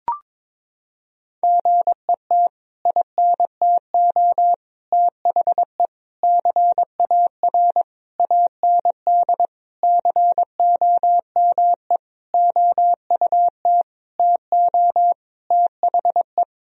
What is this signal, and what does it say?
Morse code, heard as a keyed tone, about 21 wpm, 700 Hz. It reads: GET INTO THE CAR AND COME OUT TO THE
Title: Sentences from Top 1000 Words - Review Part 1 - 22wpm